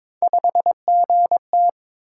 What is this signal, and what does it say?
Morse code, heard as a keyed tone, about 22 wpm, 700 Hz. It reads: 5GT
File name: practice / 22wpm / Three Letter-Number Combinations - 22wpm